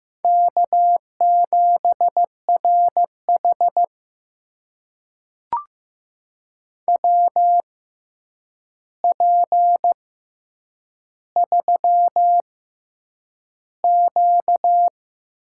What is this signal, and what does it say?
Morse code, heard as a keyed tone, about 15 wpm, spaced as for 8 wpm, 700 Hz. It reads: K7RH WP3Q
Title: USA Call Signs - 4 Characters - 6x Character Spacing - 15wpm